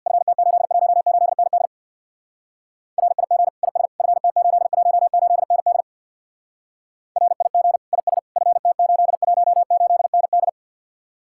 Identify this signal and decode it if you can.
Morse code, heard as a keyed tone, about 64 wpm, 700 Hz. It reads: FT818ND RIG IS FT818ND RIG IS FT818ND